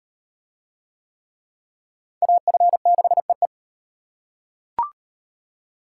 Morse code, heard as a keyed tone, about 33 wpm, 700 Hz. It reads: AF6EE